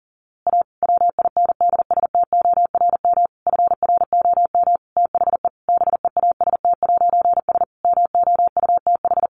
Morse code, heard as a keyed tone, about 37 wpm, 700 Hz. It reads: A WINDSTORM FROM THE BEAST'S MOUTH
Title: A Sound Of Thunder 40wpm